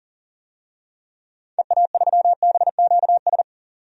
Morse code, heard as a keyed tone, about 37 wpm, 700 Hz. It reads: EA3BQS
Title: DXCC-100 Call Signs - 6 Characters - 40wpm